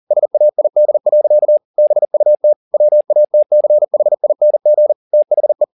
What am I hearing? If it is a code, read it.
Morse code, heard as a keyed tone, about 40 wpm, 600 Hz. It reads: SAID. BUT WATCHING THE
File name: the old man and the sea 2 40wpm